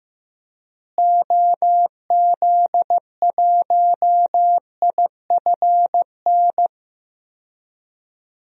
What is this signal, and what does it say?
Morse code, heard as a keyed tone, about 15 wpm, 700 Hz. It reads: OZ1IFN
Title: DXCC-100 Call Signs - 6 Characters - Repeated 3x - 15wpm